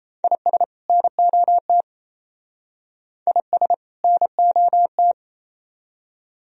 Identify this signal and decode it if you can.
Morse code, heard as a keyed tone, about 33 wpm, 700 Hz. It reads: IS NOT IS NOT
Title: Speed-Racing - Sets of 2 Words 28wpm